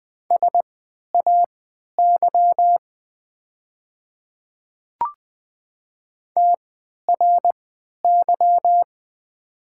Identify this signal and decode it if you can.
Morse code, heard as a keyed tone, about 20 wpm, spaced as for 11 wpm, 690 Hz. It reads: SAY TRY
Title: 3x Character Spacing - Top 200 Words - 20wpm